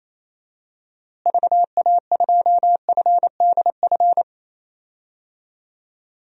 Morse code, heard as a keyed tone, about 28 wpm, 700 Hz. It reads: VA2FDF
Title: DXCC-100 Call Signs - 6 Characters - Repeated 2x - 28wpm